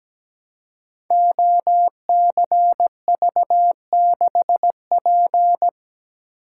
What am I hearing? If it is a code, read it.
Morse code, heard as a keyed tone, about 17 wpm, 700 Hz. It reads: OCV6P
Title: Code Groups - 5 Letter:Numbers - 17wpm